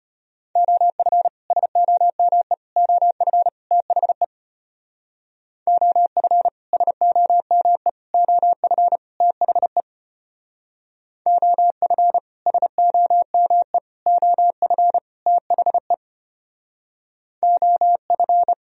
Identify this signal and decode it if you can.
Morse code, heard as a keyed tone, about 31 wpm, 700 Hz. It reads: OF SOME OF THE OF SOME OF THE OF SOME OF THE OF
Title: Speed-Racing - Sets of 4 Words 25wpm